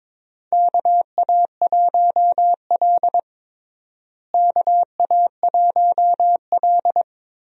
Morse code, heard as a keyed tone, about 22 wpm, 700 Hz. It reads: KA1L KA1L
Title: USA Call Signs - 4 Characters - Repeated 2x - 22wpm